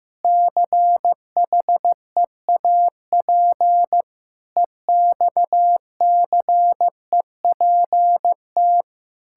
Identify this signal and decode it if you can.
Morse code, heard as a keyed tone, about 16 wpm, 700 Hz. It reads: CHEAP EXCEPT
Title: Sentences from Top 1600 Words 15wpm